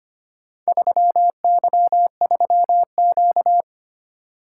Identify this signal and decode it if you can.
Morse code, heard as a keyed tone, about 25 wpm, 700 Hz. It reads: 3Y3Q
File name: practice / 25wpm / Four Letter-Number Combinations - 25wpm